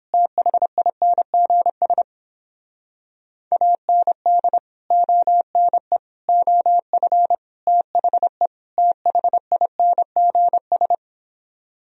Morse code, heard as a keyed tone, about 29 wpm, 700 Hz. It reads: THINGS AND ONE OF THE THINGS